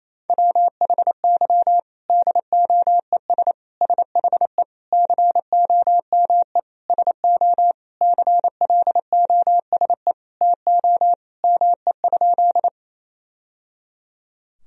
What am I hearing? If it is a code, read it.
Morse code, heard as a keyed tone, about 28 wpm, 700 Hz. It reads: WHY DOES SHE COME SO CLOSE TO ME?
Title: Sentences from Top 1000 Words - Review Part 1 - 28wpm